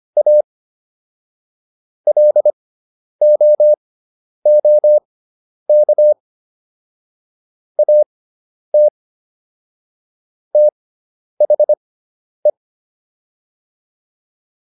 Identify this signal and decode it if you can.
Morse code, heard as a keyed tone, about 25 wpm, spaced as for 10 wpm, 600 Hz. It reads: A LOOK AT THE